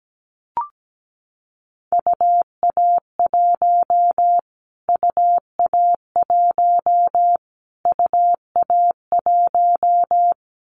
Morse code, heard as a keyed tone, about 17 wpm, 700 Hz. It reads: UA1 UA1 UA1
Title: Callsign Prefixes for Top 25 DXCC - Repeated 3x - 17wpm